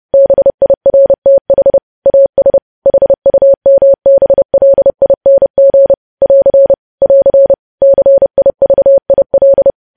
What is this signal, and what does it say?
Morse code, heard as a keyed tone, about 30 wpm, 550 Hz. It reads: BIRTH AS HUMBLING <AR> <AR> CIVIL
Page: news headlines in morse code at 30 wpm